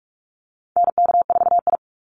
Morse code, heard as a keyed tone, about 44 wpm, 700 Hz. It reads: NK4I
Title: Call Signs Repeated 2x - 45wpm